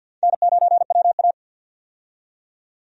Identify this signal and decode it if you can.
Morse code, heard as a keyed tone, about 49 wpm, 700 Hz. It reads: N9WA
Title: USA Call Signs - 4 Characters - 50wpm